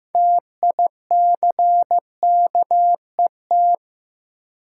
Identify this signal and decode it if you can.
Morse code, heard as a keyed tone, about 15 wpm, 700 Hz. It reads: TICKET
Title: Sentences from Top 1400 Words - New Word Drill Alt - 15wpm